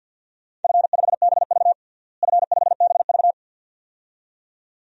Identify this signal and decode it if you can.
Morse code, heard as a keyed tone, about 52 wpm, 700 Hz. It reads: F5BV F5BV
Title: DXCC-100 Call Signs - 4 Characters - Repeated 2x - 50wpm